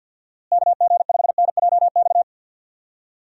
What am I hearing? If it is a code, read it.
Morse code, heard as a keyed tone, about 51 wpm, 700 Hz. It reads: KG5NJX
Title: USA Call Signs - 6 Characters - Repeated 2x - 50wpm